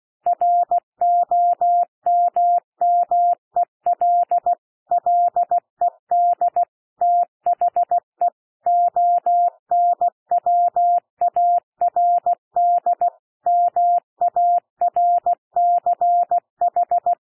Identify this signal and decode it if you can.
Morse code, heard as a keyed tone, about 16 wpm, 700 Hz. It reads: ROMMEL LED THE ONWARD MARCH
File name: cwmp3 / P1 SOUND PASSAGE 16 wpm